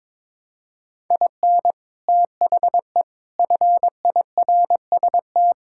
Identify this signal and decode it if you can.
Morse code, heard as a keyed tone, about 21 wpm, 700 Hz. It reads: IN THE FIRST
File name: Sentences with 7-Word Phrases - 22wpm